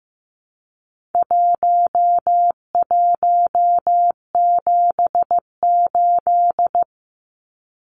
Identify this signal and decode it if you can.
Morse code, heard as a keyed tone, about 15 wpm, 700 Hz. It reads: 1178
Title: Numbers - 4-Digit 15wpm